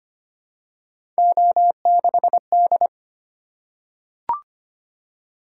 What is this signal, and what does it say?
Morse code, heard as a keyed tone, about 25 wpm, 700 Hz. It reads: O6D
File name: Three Letter-Number Combinations - 25wpm